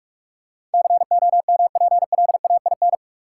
Morse code, heard as a keyed tone, about 41 wpm, 700 Hz. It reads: COMPLAIN
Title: Sentences from Top 1300 Words - New Word Drill - 45wpm